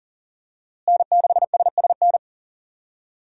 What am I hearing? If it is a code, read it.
Morse code, heard as a keyed tone, about 39 wpm, 700 Hz. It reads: N6SSN
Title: USA Call Signs - 5 Characters - Repeated 3x - 40wpm